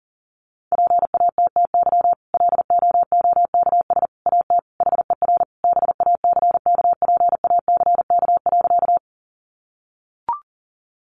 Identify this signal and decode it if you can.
Morse code, heard as a keyed tone, about 38 wpm, 700 Hz. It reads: PATTY LOOKS AT HER BACKPACK.
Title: Additional Easy Phrases 40wpm